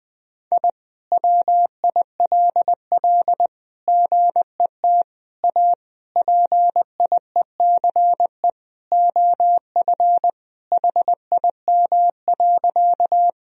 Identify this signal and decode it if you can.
Morse code, heard as a keyed tone, about 20 wpm, 700 Hz. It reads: I WILL GET A PIECE OF HIM.
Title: Sentences from Top 1000 Words - Review Part 2 - 20wpm